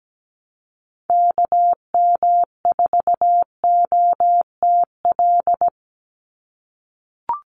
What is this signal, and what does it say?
Morse code, heard as a keyed tone, about 17 wpm, 700 Hz. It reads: KM4OTL
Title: USA Call Signs 17wpm